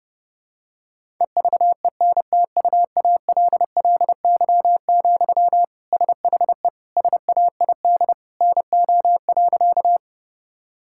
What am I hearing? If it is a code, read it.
Morse code, heard as a keyed tone, about 30 wpm, 700 Hz. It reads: EVENTUALLY, SHE SAID NO.
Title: Sentences from Top 1000 Words - Review Part 6 - 30wpm